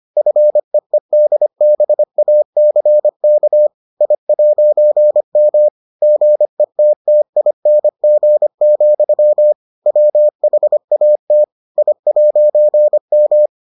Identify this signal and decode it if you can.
Morse code, heard as a keyed tone, about 25 wpm, 600 Hz. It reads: FEEDBACK I'M GETTING, WHAT I'M